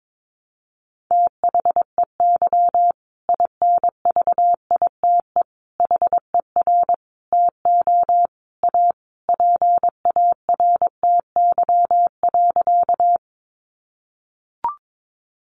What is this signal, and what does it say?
Morse code, heard as a keyed tone, about 22 wpm, 700 Hz. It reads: THEY INVITE HER TO A PARTY.
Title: Easy Phrases 22wpm